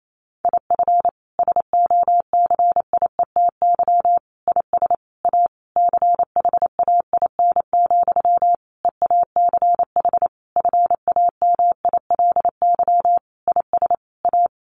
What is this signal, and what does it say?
Morse code, heard as a keyed tone, about 28 wpm, 700 Hz. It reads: IF SOCIETY IS A CHAIN, EACH FAMILY IS A
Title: Sentences from Top 1500 Words 28wpm